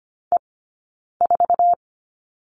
Morse code, heard as a keyed tone, about 25 wpm, 700 Hz. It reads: E 4
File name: Lesson 17d - Letter u - Two Characters 25wpm